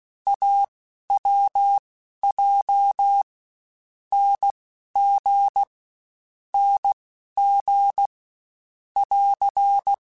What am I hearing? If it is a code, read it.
Morse code, heard as a keyed tone, about 16 wpm, spaced as for 11 wpm, 800 Hz. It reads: AWJ NG NG <AR>
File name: Short Runs / Short Runs-mp3s / Unballanced @15